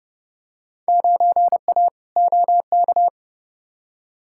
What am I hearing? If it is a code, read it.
Morse code, heard as a keyed tone, about 29 wpm, 700 Hz. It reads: 9A OK
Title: Field Day Exchanges Encore - 30wpm